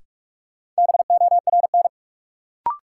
Morse code, heard as a keyed tone, about 42 wpm, 700 Hz. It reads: BORN
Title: Sentences from Top 1000 Words - New Word Drill - 45wpm